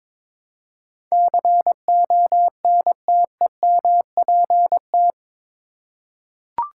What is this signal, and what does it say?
Morse code, heard as a keyed tone, about 22 wpm, 700 Hz. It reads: CONTEMPT
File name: practice / 22wpm / Eight Letter Words - 22wpm